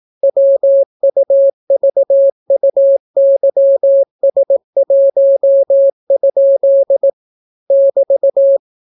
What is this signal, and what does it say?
Morse code, heard as a keyed tone, about 18 wpm, 550 Hz. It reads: WUVUYS1? <BT>